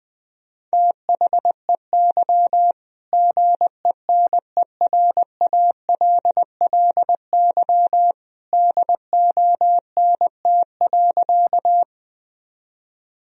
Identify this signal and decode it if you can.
Morse code, heard as a keyed tone, about 20 wpm, 700 Hz. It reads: THEY GENERALLY DONT.
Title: Sentences from Top 1000 Words - Review Part 4 - 20wpm